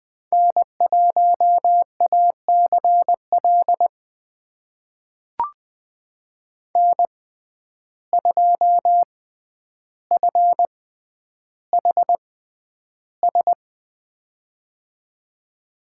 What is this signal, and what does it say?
Morse code, heard as a keyed tone, about 20 wpm, spaced as for 10 wpm, 700 Hz. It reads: N1ACL N2FHS
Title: USA Call Signs - 5 Characters - 6x Character Spacing - 20wpm